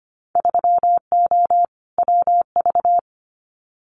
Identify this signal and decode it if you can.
Morse code, heard as a keyed tone, about 25 wpm, 700 Hz. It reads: 3O WV
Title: Winter Field Day Exchanges 25wpm